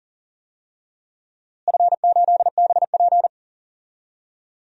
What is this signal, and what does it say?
Morse code, heard as a keyed tone, about 40 wpm, 700 Hz. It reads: F8BP
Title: DXCC-100 Call Signs - 4 Characters - Repeated 2x - 40wpm